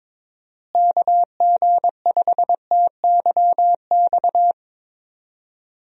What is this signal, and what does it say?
Morse code, heard as a keyed tone, about 23 wpm, 700 Hz. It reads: KG5TYX
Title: USA Call Signs - 6 Characters - Repeated 3x - 22wpm